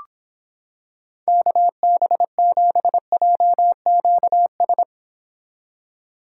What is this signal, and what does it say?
Morse code, heard as a keyed tone, about 26 wpm, 700 Hz. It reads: KB7JQS